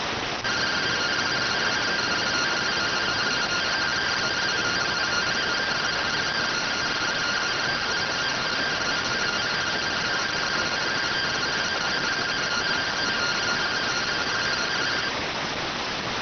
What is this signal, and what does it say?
Two-tone teleprinter signal, about 170 Hz shift.